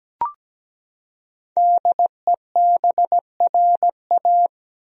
Morse code, heard as a keyed tone, about 17 wpm, 700 Hz. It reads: DEBRA
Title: USA Common Names - 17wpm